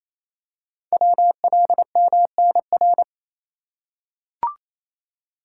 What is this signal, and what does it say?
Morse code, heard as a keyed tone, about 29 wpm, 700 Hz. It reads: WLMNR
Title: Code Groups - 5 Letters - 28wpm